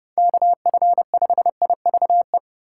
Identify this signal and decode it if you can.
Morse code, heard as a keyed tone, about 30 wpm, 700 Hz. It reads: KF5IVE
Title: USA Call Signs - 6 Characters - 3x Character Spacing - 30wpm